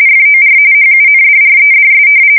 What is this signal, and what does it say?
RTTY, about 45.45 baud. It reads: FOR HUNDREDS
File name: fsk170 / battleshipnj